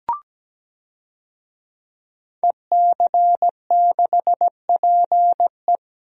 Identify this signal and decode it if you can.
Morse code, heard as a keyed tone, about 17 wpm, 700 Hz. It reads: EC6PE
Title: DXCC-100 Call Signs - 5 Characters - 17wpm